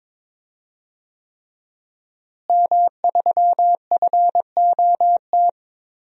Morse code, heard as a keyed tone, about 21 wpm, 700 Hz. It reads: M3FOT
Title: DXCC-100 Call Signs - 5 Characters - Repeated 3x - 22wpm